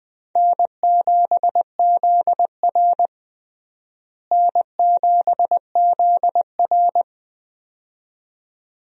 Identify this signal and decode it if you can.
Morse code, heard as a keyed tone, about 20 wpm, 700 Hz. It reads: N7ZR N7ZR
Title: USA Call Signs - 4 Characters - Repeated 2x - 20wpm